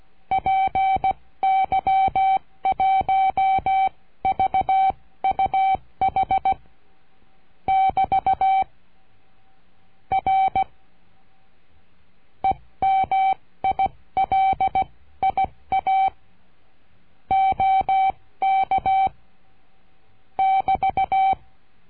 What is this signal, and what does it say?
Morse code, heard as a keyed tone, about 16 wpm, spaced as for 14 wpm, 770 Hz. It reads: PY1VUH <BT> R EMILIA OK <BT>
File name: cursos / CURSO DE TELEGRAFIA / Telegrafia - Trilha 22